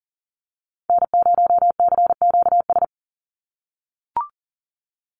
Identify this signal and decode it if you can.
Morse code, heard as a keyed tone, about 40 wpm, 700 Hz. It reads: N0CQS
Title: USA Call Signs 40wpm